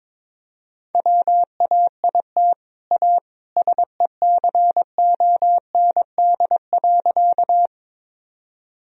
Morse code, heard as a keyed tone, about 22 wpm, 700 Hz. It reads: WAIT A SECOND.
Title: Sentences from Top 1000 Words - Review Part 1 - 22wpm